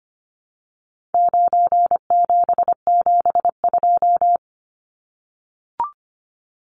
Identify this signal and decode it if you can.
Morse code, heard as a keyed tone, about 25 wpm, 700 Hz. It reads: 9772
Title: Numbers - 4-Digit 25wpm